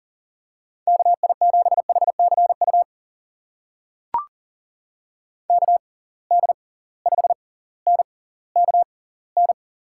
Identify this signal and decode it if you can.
Morse code, heard as a keyed tone, about 39 wpm, spaced as for 20 wpm, 700 Hz. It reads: KI7HCU KD5NKN
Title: USA Call Signs - 6 Characters - 6x Character Spacing - 40wpm